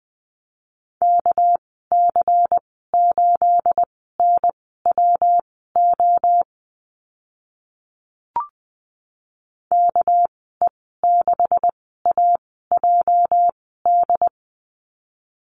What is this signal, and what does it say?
Morse code, heard as a keyed tone, about 20 wpm, spaced as for 14 wpm, 700 Hz. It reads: KC8NWO KE6AJD
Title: 219 Lesson 35g - Number 0 - Call Signs 20wpm